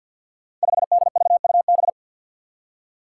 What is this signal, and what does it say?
Morse code, heard as a keyed tone, about 51 wpm, 700 Hz. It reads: 5DVUB
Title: Five Challenging Letter-Number Combinations 50wpm